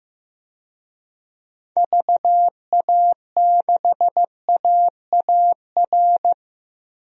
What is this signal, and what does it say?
Morse code, heard as a keyed tone, about 15 wpm, 700 Hz. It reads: VA6AAR
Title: DXCC-100 Call Signs - 6 Characters - 15wpm